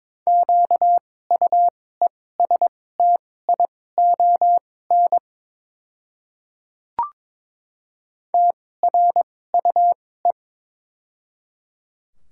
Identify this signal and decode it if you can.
Morse code, heard as a keyed tone, about 23 wpm, spaced as for 16 wpm, 700 Hz. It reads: QUESTION TRUE
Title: 2x Character Spacing - Top 500 Words - 22wpm